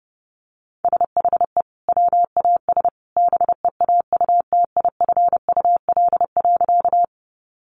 Morse code, heard as a keyed tone, about 30 wpm, 700 Hz. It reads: SHE WAS BEAUTIFUL.
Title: Sentences from Top 900 Words 30wpm